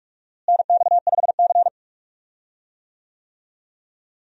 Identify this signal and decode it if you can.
Morse code, heard as a keyed tone, about 44 wpm, 700 Hz. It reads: NX5C
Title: USA Call Signs - 4 Characters - Repeated 2x - 45wpm